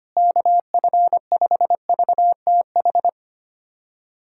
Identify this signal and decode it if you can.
Morse code, heard as a keyed tone, about 25 wpm, 700 Hz. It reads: KF5VTH
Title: USA Callsign DE Callsign - 6 Characters - 25wpm